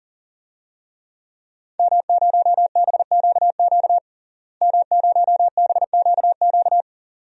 Morse code, heard as a keyed tone, about 37 wpm, 700 Hz. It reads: M0BQQ M0BQQ
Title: DXCC-100 Call Signs - 5 Characters - Repeated 2x - 40wpm